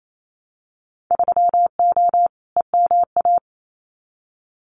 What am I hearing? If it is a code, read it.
Morse code, heard as a keyed tone, about 28 wpm, 700 Hz. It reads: 3O EMA
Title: Winter Field Day Exchanges Encore 28wpm